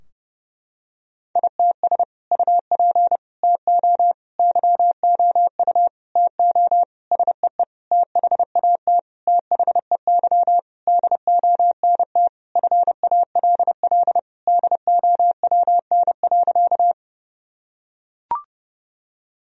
Morse code, heard as a keyed tone, about 30 wpm, 700 Hz. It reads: ITS UP TO YOU TO SEE THAT THEY DONT FALL DOWN.
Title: Sentences from Top 1000 Words - Review Part 2 - 30wpm